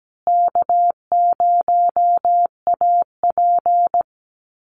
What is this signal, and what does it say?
Morse code, heard as a keyed tone, about 17 wpm, 700 Hz. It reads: K0AP
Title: Call Signs Repeated 2x - 17wpm